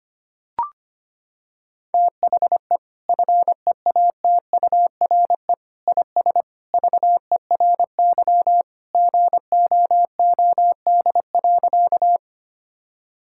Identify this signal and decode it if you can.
Morse code, heard as a keyed tone, about 25 wpm, 700 Hz. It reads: THE FEATURE IS VERY GOOD.
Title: Sentences from Top 1000 Words - Review Part 5 - 25wpm